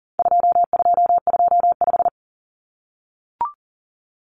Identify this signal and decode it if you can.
Morse code, heard as a keyed tone, about 40 wpm, 700 Hz. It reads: 2225
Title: Common Cut Numbers - 4-Digit 40wpm